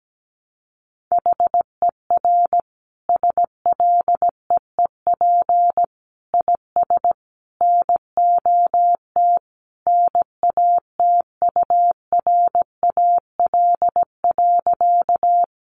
Morse code, heard as a keyed tone, about 17 wpm, 700 Hz. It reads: HER SLEEP IS NOT NATURAL.
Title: Sentences from Top 700 Words 17wpm